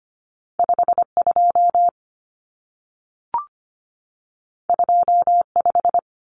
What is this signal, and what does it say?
Morse code, heard as a keyed tone, about 25 wpm, 700 Hz. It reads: 52 25
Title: Numbers - 2-Digit 25wpm